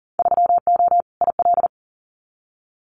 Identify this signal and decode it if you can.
Morse code, heard as a keyed tone, about 40 wpm, 700 Hz. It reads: 3O IL
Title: Winter Field Day Exchanges 40wpm